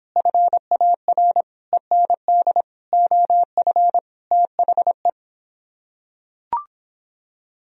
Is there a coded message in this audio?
Yes: Morse code from 0.2 to 5.1 s, heard as a keyed tone, about 26 wpm, 700 Hz. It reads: FAR END OF THE